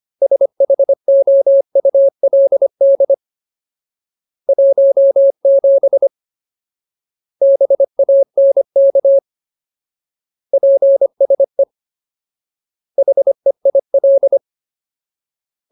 Morse code, heard as a keyed tone, about 25 wpm, 550 Hz. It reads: SHOULD 17 BANK PSE HEIL
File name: Ultimate Word List / 25WPM Ultimate